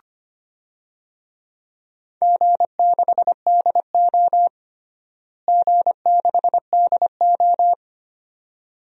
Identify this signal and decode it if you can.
Morse code, heard as a keyed tone, about 25 wpm, 700 Hz. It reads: G6DO G6DO
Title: DXCC-100 Call Signs - 4 Characters - Repeated 2x - 25wpm